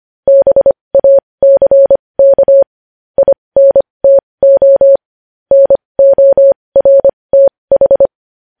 Morse code, heard as a keyed tone, about 25 wpm, spaced as for 20 wpm, 550 Hz. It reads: BACK INTO NORTH
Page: News Headlines in Morse Code at 20 WPM